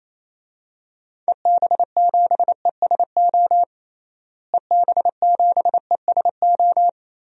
Morse code, heard as a keyed tone, about 27 wpm, 700 Hz. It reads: EB7ESO EB7ESO
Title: DXCC-100 Call Signs - 6 Characters - Repeated 2x - 28wpm